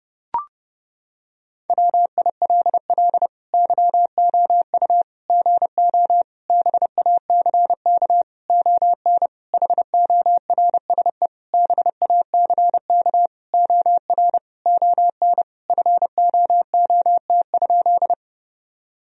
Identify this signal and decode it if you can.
Morse code, heard as a keyed tone, about 32 wpm, 700 Hz. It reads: WILL YOU GO BACK ON HORSE BACK OR ON FOOT?
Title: Sentences from Top 1000 Words - Review Part 3 - 30wpm